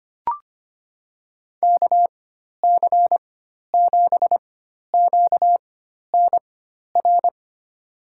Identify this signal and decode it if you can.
Morse code, heard as a keyed tone, about 25 wpm, spaced as for 12 wpm, 700 Hz. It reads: KC7QNR
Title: USA Call Signs - 6 Characters - 4x Character Spacing - 25wpm